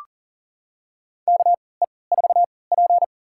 Morse code, heard as a keyed tone, about 40 wpm, spaced as for 23 wpm, 700 Hz. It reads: KE4P
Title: USA Call Signs - 3x Character Spacing - 40wpm